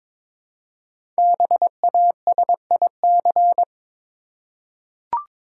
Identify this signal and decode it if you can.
Morse code, heard as a keyed tone, about 22 wpm, 700 Hz. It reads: BASIC
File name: 5 Letter Words - First 3 Letters Given - 22wpm